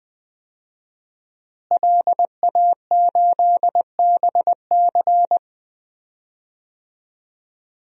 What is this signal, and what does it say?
Morse code, heard as a keyed tone, about 20 wpm, 700 Hz. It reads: LA8BC
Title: DXCC-100 Call Signs - 5 Characters - Repeated 2x - 20wpm